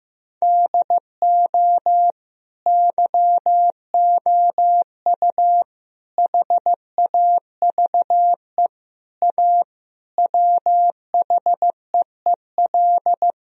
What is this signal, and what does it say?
Morse code, heard as a keyed tone, about 15 wpm, 700 Hz. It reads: DO YOU HAVE A WHEEL
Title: Sentences from Top 1000 Words - Review Part 3 - 15wpm